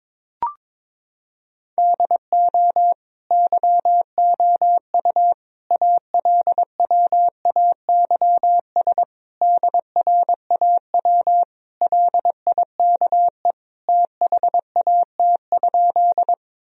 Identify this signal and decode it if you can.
Morse code, heard as a keyed tone, about 22 wpm, 700 Hz. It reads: DO YOU ALWAYS DRAW LIKE THAT?
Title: Sentences from Top 1000 Words - Review Part 1 - 22wpm